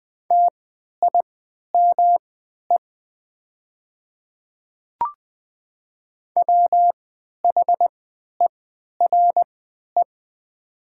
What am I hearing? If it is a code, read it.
Morse code, heard as a keyed tone, about 20 wpm, spaced as for 11 wpm, 700 Hz. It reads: TIME WHERE
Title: 3x Character Spacing - Top 200 Words - 20wpm